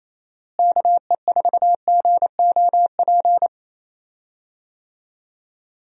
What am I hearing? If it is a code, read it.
Morse code, heard as a keyed tone, about 28 wpm, 700 Hz. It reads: KE4GOP
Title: USA Call Signs - 6 Characters - Repeated 3x - 28wpm